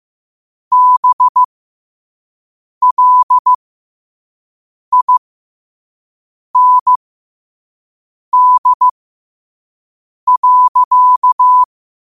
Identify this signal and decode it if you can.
Morse code, heard as a keyed tone, about 15 wpm, spaced as for 5 wpm, 1000 Hz. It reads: BLIND.